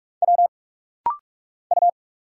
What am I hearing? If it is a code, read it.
Morse code, heard as a keyed tone, about 46 wpm, 700 Hz. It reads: W U
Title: Single Letter-Number Alt - Rapid-Fire 45wpm